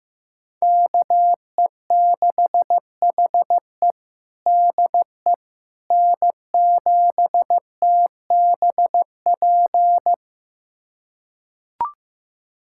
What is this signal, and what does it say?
Morse code, heard as a keyed tone, about 15 wpm, 700 Hz. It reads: KE6HE DE N7TBP
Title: USA Callsign DE Callsign - 5 Characters - 15wpm